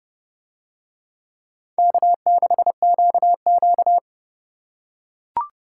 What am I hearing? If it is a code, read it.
Morse code, heard as a keyed tone, about 28 wpm, 690 Hz. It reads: K6QQ